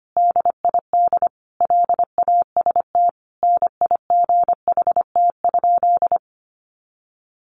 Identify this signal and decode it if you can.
Morse code, heard as a keyed tone, about 25 wpm, 700 Hz. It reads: DID LAST NIGHT?
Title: Sentences from Top 300 Words 25wpm